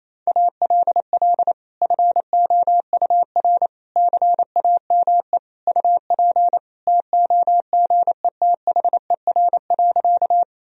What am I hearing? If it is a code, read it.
Morse code, heard as a keyed tone, about 28 wpm, 700 Hz. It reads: ALL FOUR CAME UP TOGETHER.
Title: Sentences from Top 1000 Words - Review Part 1 - 28wpm